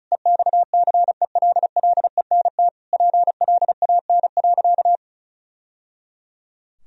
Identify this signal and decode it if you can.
Morse code, heard as a keyed tone, about 36 wpm, 700 Hz. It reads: EXCELLENT PLAN.
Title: Sentences from Top 1000 Words - Review Part 2 - 35wpm